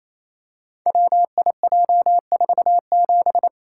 Morse code, heard as a keyed tone, about 29 wpm, 700 Hz. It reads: WIJ47
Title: Code Groups - 5 Letter:Numbers - 28wpm